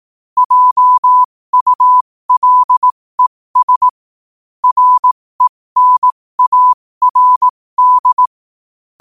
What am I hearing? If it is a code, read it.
Morse code, heard as a keyed tone, about 18 wpm, spaced as for 15 wpm, 1000 Hz. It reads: JULES RENARD
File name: qotdfiles / 15wpm